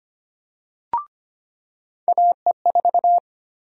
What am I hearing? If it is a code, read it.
Morse code, heard as a keyed tone, about 25 wpm, 700 Hz. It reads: AE4
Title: USA Call Sign Prefixes - 25wpm